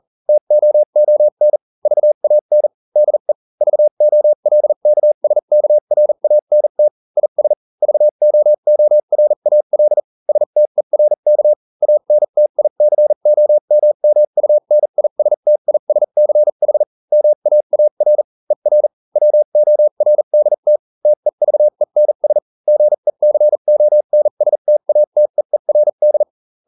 Morse code, heard as a keyed tone, about 40 wpm, spaced as for 35 wpm, 600 Hz. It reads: TOON VAN DE VOLKSKRANT IS VOORAL STERK ANTICOMMUNISTISCH MAAR ER WORDT TEVENS GECONSTATEERD